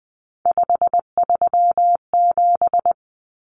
Morse code, heard as a keyed tone, about 20 wpm, 700 Hz. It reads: 537
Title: Common Cut Numbers - 3-Digit 20wpm